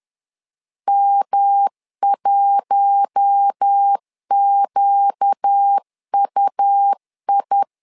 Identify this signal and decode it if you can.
Morse code, heard as a keyed tone, about 10 wpm, 800 Hz. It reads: M1QUI
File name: LEZIONI / 4 Q1S 10wpm 2133